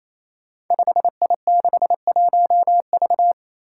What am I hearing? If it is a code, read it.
Morse code, heard as a keyed tone, about 28 wpm, 700 Hz. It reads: 5I61V
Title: Five Challenging Letter-Number Combinations 28wpm